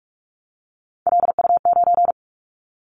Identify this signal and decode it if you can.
Morse code, heard as a keyed tone, about 46 wpm, 700 Hz. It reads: LU9
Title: Callsign Prefixes for Top 50 DXCC - 45wpm